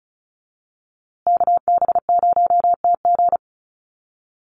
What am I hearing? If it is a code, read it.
Morse code, heard as a keyed tone, about 34 wpm, 700 Hz. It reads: KB0TG
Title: USA Call Signs Repeated 3x 35wpm